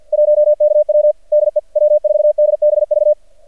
Morse code, heard as a keyed tone, about 50 wpm, 600 Hz. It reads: 1KW DE W4DBV
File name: archived / CRSnet